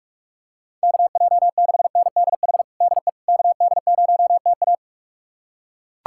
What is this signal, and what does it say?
Morse code, heard as a keyed tone, about 42 wpm, 700 Hz. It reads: KJ6NDH DE KD0TA
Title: USA Callsign DE Callsign 45wpm